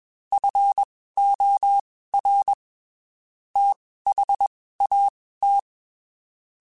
Morse code, heard as a keyed tone, about 21 wpm, spaced as for 15 wpm, 800 Hz. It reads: FOR THAT
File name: (spaced) / QSOs Spaced-mp3s / WA7QOX 1 @20 sp